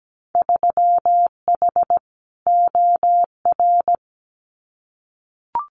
Morse code, heard as a keyed tone, about 17 wpm, 700 Hz. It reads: 3H OR
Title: Winter Field Day Exchanges Encore 17wpm